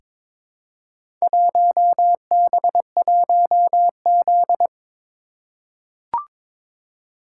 Morse code, heard as a keyed tone, about 22 wpm, 700 Hz. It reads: 1B1Z
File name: practice / 22wpm / Four Letter-Number Combinations - 22wpm